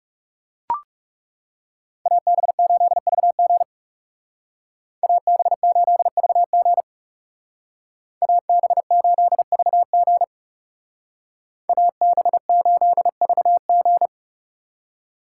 Morse code, heard as a keyed tone, about 35 wpm, 700 Hz. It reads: AB8VG AB8VG AB8VG AB8VG
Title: Speed Racing - USA 2x2 Call Signs 30wpm